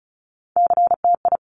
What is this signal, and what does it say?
Morse code, heard as a keyed tone, about 36 wpm, 700 Hz. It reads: CTI
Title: Top Three Letter Combinations in Words 35wpm